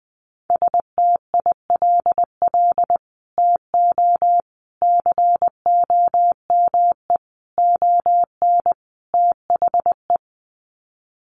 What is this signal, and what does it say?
Morse code, heard as a keyed tone, about 20 wpm, 700 Hz. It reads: STILL TO COME ON THE
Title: Sets of 5 Words Encore2 20wpm